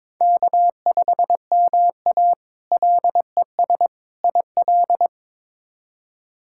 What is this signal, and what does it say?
Morse code, heard as a keyed tone, about 22 wpm, 700 Hz. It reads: K5MA LES IL
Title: K1USN SST QSO Practice - 22wpm